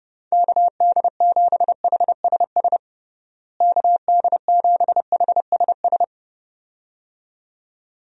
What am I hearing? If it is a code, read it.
Morse code, heard as a keyed tone, about 32 wpm, 700 Hz. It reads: KD7HSS KD7HSS
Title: USA Call Signs - 6 Characters - Repeated 2x - 30wpm